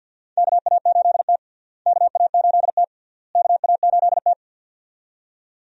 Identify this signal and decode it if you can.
Morse code, heard as a keyed tone, about 46 wpm, 700 Hz. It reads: KA8T KA8T KA8T
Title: USA Call Signs - 4 Characters - Repeated 3x - 50wpm